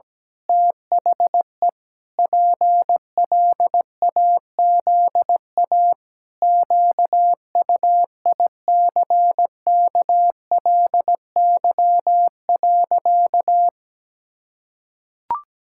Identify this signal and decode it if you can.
Morse code, heard as a keyed tone, about 17 wpm, 700 Hz. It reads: THE PLAZA QUICKLY.
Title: Pangram Challenge v2 - 17wpm